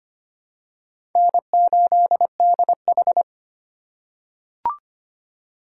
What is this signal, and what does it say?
Morse code, heard as a keyed tone, about 25 wpm, 700 Hz. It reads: N8DH
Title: USA Call Signs - 4 Characters - 4x Character Spacing - 25wpm